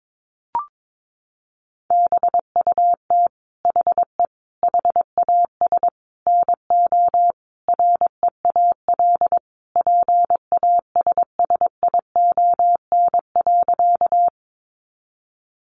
Morse code, heard as a keyed tone, about 22 wpm, 700 Hz. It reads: BUT HE HAS NO REAL PASSION.
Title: Sentences from Top 1300 Words 22wpm